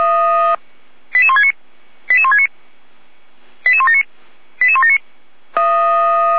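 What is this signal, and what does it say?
Five-tone selective call (ZVEI), heard as quick runs of five tones, tones of about 70 ms. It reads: 88169 88169 88179 88179
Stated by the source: ZVEI five-tone selective call received by SDR.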